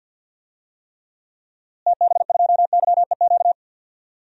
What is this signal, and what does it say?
Morse code, heard as a keyed tone, about 52 wpm, 700 Hz. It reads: TB2YEQ
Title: DXCC-100 Call Signs - 6 Characters - Repeated 2x - 50wpm